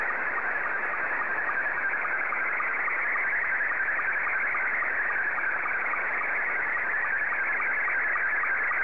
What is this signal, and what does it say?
SSTV (slow-scan television), Scottie 2.
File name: sstv / S2-JL1FQZ